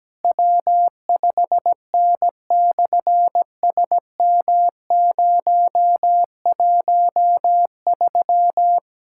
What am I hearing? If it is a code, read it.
Morse code, heard as a keyed tone, about 17 wpm, 700 Hz. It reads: W5N/SM013
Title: SOTA QSO Practice 17wpm